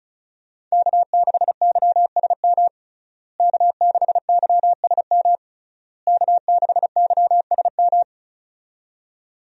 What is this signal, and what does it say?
Morse code, heard as a keyed tone, about 35 wpm, 700 Hz. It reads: K6YSM K6YSM K6YSM
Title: USA Call Signs - 5 Characters - Repeated 3x - 35wpm